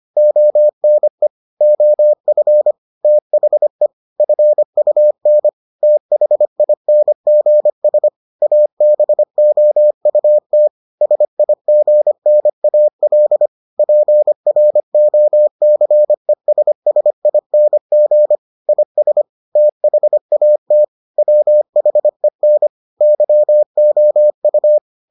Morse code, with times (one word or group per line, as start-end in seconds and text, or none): 0.2-1.3 ONE
1.6-2.7 OF
3.0-3.9 THE
4.2-5.5 FUN
5.8-8.1 THINGS
8.4-10.7 ABOUT
11.0-13.5 SIGNAL
13.8-18.4 PROCESSING
18.7-19.2 IS
19.6-20.8 THAT
21.2-22.7 WHEN
23.0-24.8 YOU